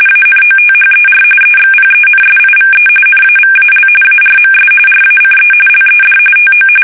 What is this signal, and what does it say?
RTTY, about 45.45 baud. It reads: ANOTHER EXPENSIVE NAVAL ARMS RACE LED TO
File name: fsk850 / battleshipnj